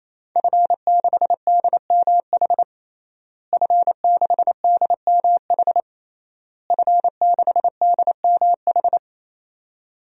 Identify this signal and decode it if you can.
Morse code, heard as a keyed tone, about 28 wpm, 700 Hz. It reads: F6DMH F6DMH F6DMH
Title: DXCC-100 Call Signs - 5 Characters - Repeated 3x - 28wpm